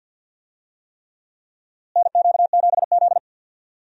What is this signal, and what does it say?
Morse code, heard as a keyed tone, about 48 wpm, 700 Hz. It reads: NQ7Z
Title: USA Call Signs - 4 Characters - Repeated 3x - 50wpm